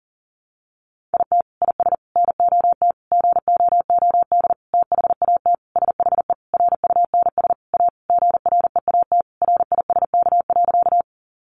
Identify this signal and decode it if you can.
Morse code, heard as a keyed tone, about 37 wpm, 700 Hz. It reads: IT IS NOT GOOD THAT SHE RUNS A GREAT RISK.
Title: Sentences from Top 600 Words 40wpm